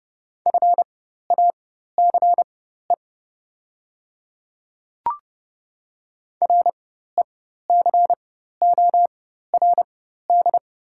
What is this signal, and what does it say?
Morse code, heard as a keyed tone, about 29 wpm, spaced as for 14 wpm, 690 Hz. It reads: FACE RECORD
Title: 4x Character Spacing - Top 500 Words - 30wpm